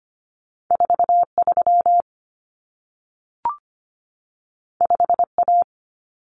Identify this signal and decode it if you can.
Morse code, heard as a keyed tone, about 25 wpm, 700 Hz. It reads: 43 5A
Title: Common Cut Numbers - 2-Digit 25wpm